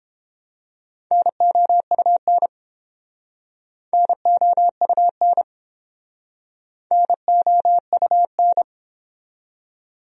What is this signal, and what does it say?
Morse code, heard as a keyed tone, about 30 wpm, 700 Hz. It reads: NOUN NOUN NOUN